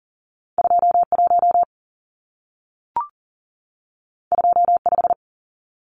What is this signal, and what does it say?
Morse code, heard as a keyed tone, about 40 wpm, 700 Hz. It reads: 21 25
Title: Numbers - 2-Digit 40wpm